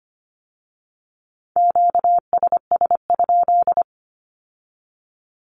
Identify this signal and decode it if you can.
Morse code, heard as a keyed tone, about 25 wpm, 700 Hz. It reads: QSS?
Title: Q-Codes with Spoken Meaning - 25wpm